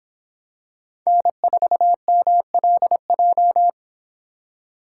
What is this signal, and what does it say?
Morse code, heard as a keyed tone, about 25 wpm, 700 Hz. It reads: N4MLJ